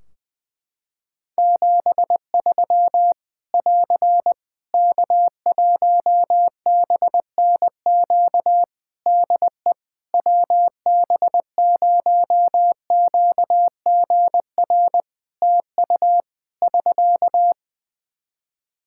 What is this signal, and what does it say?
Morse code, heard as a keyed tone, about 20 wpm, 700 Hz. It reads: 73 <AR> K1BNQ DE WB0QGR TU <SK>
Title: Practice QSO - Exchange 4 - Ending - 20wpm